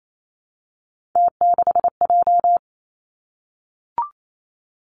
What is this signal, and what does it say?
Morse code, heard as a keyed tone, about 28 wpm, 700 Hz. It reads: T6J
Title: Three Challenging Letter-Number Combinations 28wpm